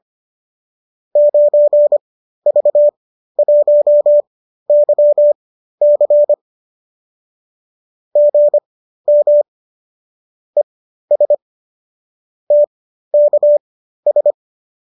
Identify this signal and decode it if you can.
Morse code, heard as a keyed tone, about 25 wpm, spaced as for 13 wpm, 590 Hz. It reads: 9V1YC GM ES TKS
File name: qso207 13